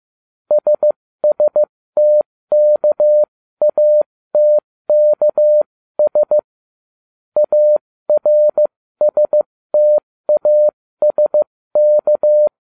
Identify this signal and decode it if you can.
Morse code, heard as a keyed tone, about 15 wpm, spaced as for 13 wpm, 600 Hz. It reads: SSTKATKS ARSTASK